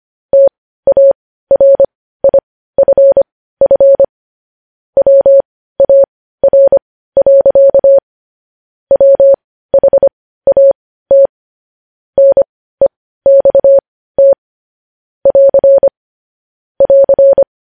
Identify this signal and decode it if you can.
Morse code, heard as a keyed tone, about 25 wpm, spaced as for 15 wpm, 550 Hz. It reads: TARIFF WAR. WHAT NEXT <AR> <AR>
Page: news-headlines-in-morse-code-at-15-wpm